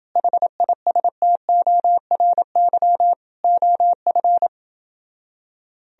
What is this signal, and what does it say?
Morse code, heard as a keyed tone, about 28 wpm, 700 Hz. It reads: HISTORY OF